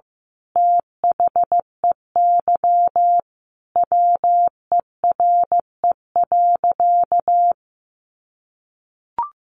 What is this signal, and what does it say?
Morse code, heard as a keyed tone, about 15 wpm, 700 Hz. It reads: THEY WERE.
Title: Sentences from Top 1000 Words 15wpm